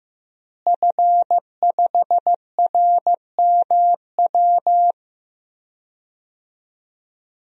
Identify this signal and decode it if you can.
Morse code, heard as a keyed tone, about 15 wpm, 700 Hz. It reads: F5RMW
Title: DXCC-100 Call Signs - 5 Characters - Repeated 3x - 15wpm